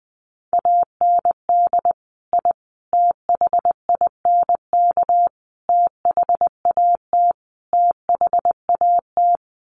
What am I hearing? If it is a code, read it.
Morse code, heard as a keyed tone, about 20 wpm, 700 Hz. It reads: AND I THINK THAT THAT
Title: Sets of 5 Words Encore 20wpm